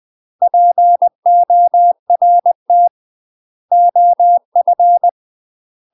Morse code, heard as a keyed tone, about 20 wpm, 700 Hz. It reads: PORT OF
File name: UK MASTER QSOs TRAINING / 20WPM / UK MASTER QSOs 0036